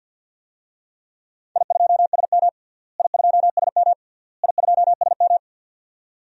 Morse code, heard as a keyed tone, about 52 wpm, 700 Hz. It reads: I2SM I2SM I2SM
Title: DXCC-100 Call Signs - 4 Characters - Repeated 3x - 50wpm